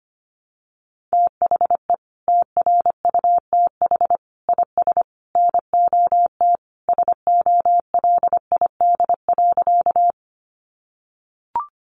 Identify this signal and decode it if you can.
Morse code, heard as a keyed tone, about 25 wpm, 700 Hz. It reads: THE TRUTH IS NOT SOLID.
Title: Sentences from Top 1500 Words 25wpm